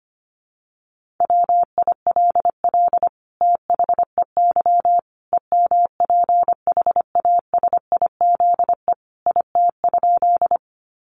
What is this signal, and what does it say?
Morse code, heard as a keyed tone, about 25 wpm, 700 Hz. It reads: WILL THEY EMPHASIZE IT?
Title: Sentences from Top 1500 Words 25wpm